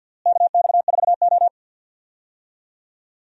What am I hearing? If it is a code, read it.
Morse code, heard as a keyed tone, about 53 wpm, 690 Hz. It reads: KX4O
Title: USA Call Signs - 4 Characters - Repeated 2x - 50wpm